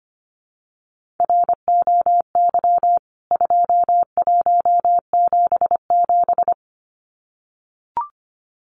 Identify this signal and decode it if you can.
Morse code, heard as a keyed tone, about 25 wpm, 700 Hz. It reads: ROY 2177
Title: CWT Exchange 25wpm